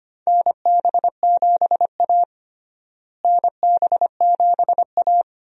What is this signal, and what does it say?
Morse code, heard as a keyed tone, about 25 wpm, 700 Hz. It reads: NB7A NB7A
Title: USA Call Signs - 4 Characters - Repeated 2x - 25wpm